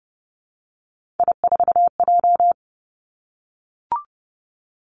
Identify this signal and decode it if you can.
Morse code, heard as a keyed tone, about 30 wpm, 700 Hz. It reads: I4J
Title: Three Challenging Letter-Number Combinations 30wpm